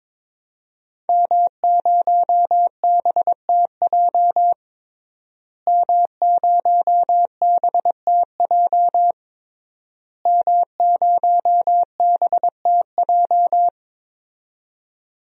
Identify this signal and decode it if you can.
Morse code, heard as a keyed tone, about 22 wpm, 700 Hz. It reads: M0BTJ M0BTJ M0BTJ
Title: DXCC-100 Call Signs - 5 Characters - Repeated 3x - 22wpm